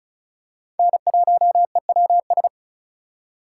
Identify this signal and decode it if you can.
Morse code, heard as a keyed tone, about 36 wpm, 700 Hz. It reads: N1EWS
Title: USA Call Signs - 5 Characters - 3x Character Spacing - 35wpm